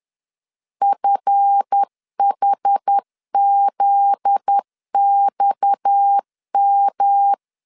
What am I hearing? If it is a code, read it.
Morse code, heard as a keyed tone, about 10 wpm, 800 Hz. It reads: FHZXM
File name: CW / DET / 1 9 1500C S C 10wpm 4557